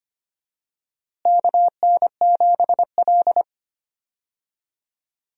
Morse code, heard as a keyed tone, about 25 wpm, 700 Hz. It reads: KN7L
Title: USA Call Signs - 4 Characters - Repeated 3x - 25wpm